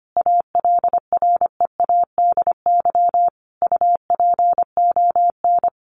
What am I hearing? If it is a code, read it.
Morse code, heard as a keyed tone, about 25 wpm, 700 Hz. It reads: ALREADY UPON
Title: Sentences from Top 1500 Words 25wpm